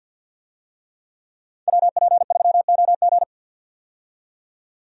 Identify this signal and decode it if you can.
Morse code, heard as a keyed tone, about 52 wpm, 700 Hz. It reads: WP3OG
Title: USA Call Signs - 5 Characters - 50wpm